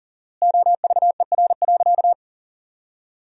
Morse code, heard as a keyed tone, about 40 wpm, 700 Hz. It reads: OVER.
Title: 4x Word-Spacing - Sentences from Top 500 Words - 40wpm